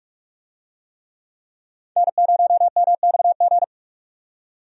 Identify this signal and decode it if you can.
Morse code, heard as a keyed tone, about 42 wpm, 700 Hz. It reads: N0MXG
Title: USA Call Signs - 5 Characters - 45wpm